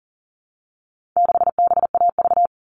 Morse code, heard as a keyed tone, about 40 wpm, 700 Hz. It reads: 6BAV
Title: All Cut Numbers - 4-Digit 40wpm